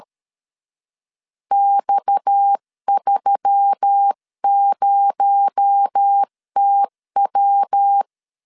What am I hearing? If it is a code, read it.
Morse code, heard as a keyed tone, about 13 wpm, 800 Hz. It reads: X30TW
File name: CW / DET / 1 6 1500C S C 12wpm 3533